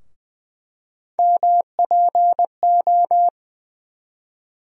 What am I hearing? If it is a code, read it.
Morse code, heard as a keyed tone, about 20 wpm, 700 Hz. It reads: MPO
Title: USA Call Sign Postfixes - 3 Letters 20wpm